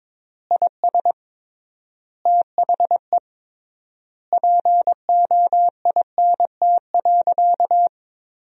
Morse code, heard as a keyed tone, about 22 wpm, 700 Hz. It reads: IS THE POINT.
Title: 3x Word-Spacing - Sentences from Top 200 Words - 22wpm